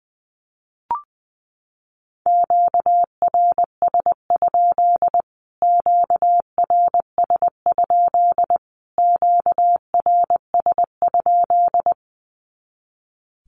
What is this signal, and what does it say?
Morse code, heard as a keyed tone, about 20 wpm, 700 Hz. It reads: QRS? QRS? QRS?
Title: Q-Codes Repeated 3x 20wpm